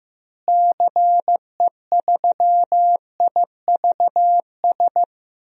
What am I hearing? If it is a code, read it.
Morse code, heard as a keyed tone, about 15 wpm, 700 Hz. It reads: CE3IVS
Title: DXCC-100 Call Signs - 6 Characters - Repeated 2x - 15wpm